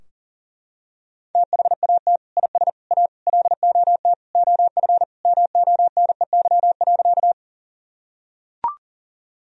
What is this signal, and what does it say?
Morse code, heard as a keyed tone, about 40 wpm, 700 Hz. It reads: THAT IS A LOT OF MONEY.
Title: Sentences from Top 1000 Words - Review Part 3 - 40wpm